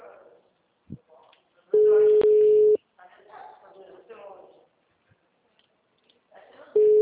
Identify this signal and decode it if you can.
Ringback tone (Europe, 425 Hz), with speech.
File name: out- -2013- - - .36756